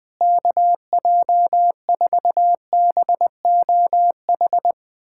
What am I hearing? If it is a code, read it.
Morse code, heard as a keyed tone, about 20 wpm, 700 Hz. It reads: KJ4BOH
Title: USA Callsign DE Callsign - 6 Characters - 20wpm